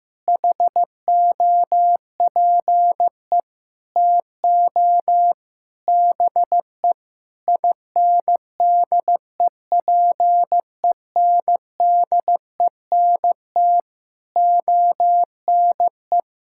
Morse code, heard as a keyed tone, about 15 wpm, 700 Hz. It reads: HOPE TO BE INDEPENDENT ONE
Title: Sentences from Top 1000 Words - Review Part 6 - 15wpm